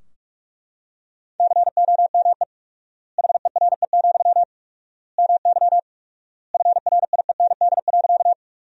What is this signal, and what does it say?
Morse code, heard as a keyed tone, about 41 wpm, 700 Hz. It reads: COME HERE, MY FRIEND.
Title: 4x Word-Spacing - Sentences from Top 300 Words - 45wpm